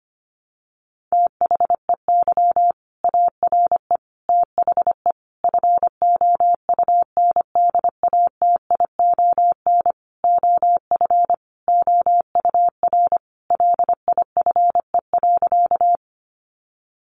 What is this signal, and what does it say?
Morse code, heard as a keyed tone, about 25 wpm, 700 Hz. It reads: THEY ARE THE FOUNDATION OF OUR LIFE.
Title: Sentences from Top 1200 Words 25wpm